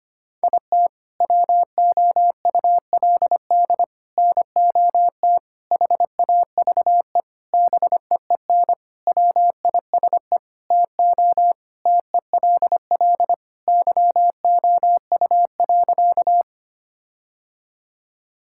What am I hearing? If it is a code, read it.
Morse code, heard as a keyed tone, about 25 wpm, 700 Hz. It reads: IT WOULD NOT HAVE BEEN WISE TO TELL YOU.
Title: Sentences from Top 1600 Words 25wpm